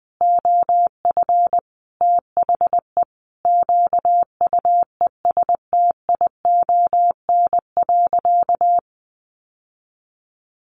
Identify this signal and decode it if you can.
Morse code, heard as a keyed tone, about 20 wpm, 700 Hz. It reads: OF THE QUESTION.
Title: Sentences from Top 1500 Words 20wpm